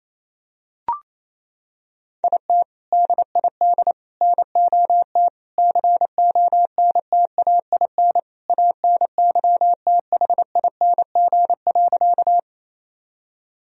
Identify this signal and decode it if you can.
Morse code, heard as a keyed tone, about 28 wpm, 700 Hz. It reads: IT DID NOT CONTAIN ANYTHING.
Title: Sentences from Top 1000 Words - Review Part 2 - 28wpm